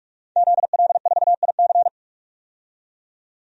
Morse code, heard as a keyed tone, about 46 wpm, 700 Hz. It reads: ZL4IC
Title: DXCC-100 Call Signs - 5 Characters - 45wpm